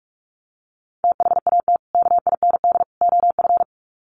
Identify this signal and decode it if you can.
Morse code, heard as a keyed tone, about 44 wpm, 700 Hz. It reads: THAT KIND OF
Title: Sets of 3 Words 45wpm